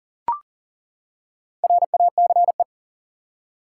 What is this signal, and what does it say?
Morse code, heard as a keyed tone, about 40 wpm, 700 Hz. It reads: RACE
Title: Four Letter Words - 40wpm